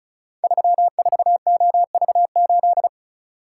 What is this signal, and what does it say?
Morse code, heard as a keyed tone, about 36 wpm, 690 Hz. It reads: 34OV8
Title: Code Groups - 5 Letter:Numbers - 35wpm